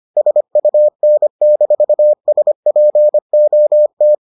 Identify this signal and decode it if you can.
Morse code, heard as a keyed tone, about 25 wpm, 600 Hz. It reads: SUN-SPOT